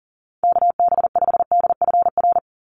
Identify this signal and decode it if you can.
Morse code, heard as a keyed tone, about 38 wpm, 700 Hz. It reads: KB5DFR
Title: USA Call Signs 40wpm